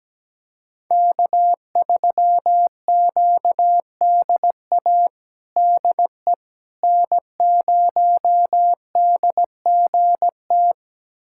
Morse code, heard as a keyed tone, about 17 wpm, 700 Hz. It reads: K3QDA DE N0DGT
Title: USA Callsign DE Callsign - 5 Characters - 17wpm